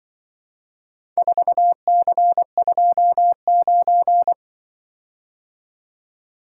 Morse code, heard as a keyed tone, about 24 wpm, 700 Hz. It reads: 4C29
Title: Speed-Racing - Four Letter-Number Combinations - 20wpm